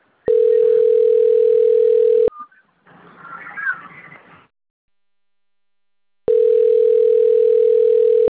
Ringback tone (North America), 440 + 480 Hz.